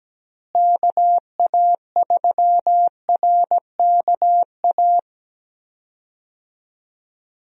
Morse code, heard as a keyed tone, about 17 wpm, 700 Hz. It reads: KA3RKA
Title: USA Call Signs - 6 Characters - Repeated 2x - 17wpm